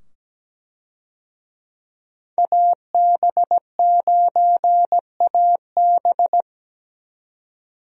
Morse code, heard as a keyed tone, about 17 wpm, 700 Hz. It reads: AB9AB
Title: USA Call Signs - 5 Characters - 17wpm